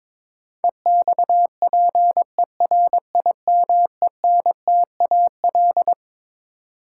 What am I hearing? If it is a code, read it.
Morse code, heard as a keyed tone, about 22 wpm, 700 Hz. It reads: EXPERIMENTAL
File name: Twelve Letter Words - 22wpm